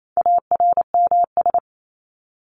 Morse code, heard as a keyed tone, about 28 wpm, 700 Hz. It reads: ARMS
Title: ES:S Suffix - Top 100 Words with Base Words 28wpm